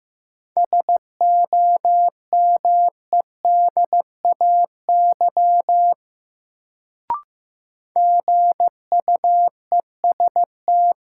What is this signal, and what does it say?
Morse code, heard as a keyed tone, about 15 wpm, 700 Hz. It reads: SOMEDAY GUEST
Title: Sentences from Top 1600 Words - New Word Drill Alt - 15wpm